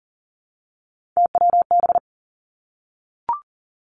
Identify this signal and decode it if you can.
Morse code, heard as a keyed tone, about 42 wpm, 700 Hz. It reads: TWB
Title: Three Challenging Letter-Number Combinations 40wpm